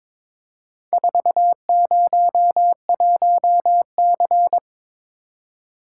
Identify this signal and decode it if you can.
Morse code, heard as a keyed tone, about 22 wpm, 700 Hz. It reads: 401C
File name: practice / 22wpm / Four Letter-Number Combinations - 22wpm